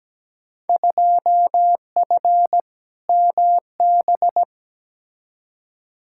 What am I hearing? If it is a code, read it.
Morse code, heard as a keyed tone, about 17 wpm, 700 Hz. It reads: 2F MB
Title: Field Day Exchanges Encore - 17wpm